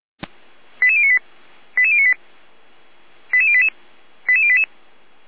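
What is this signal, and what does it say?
Five-tone selective call (ZVEI), heard as quick runs of five tones, tones of about 70 ms. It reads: 88098 88098 88088 88088
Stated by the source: ZVEI five-tone selective call 88098, received by SDR.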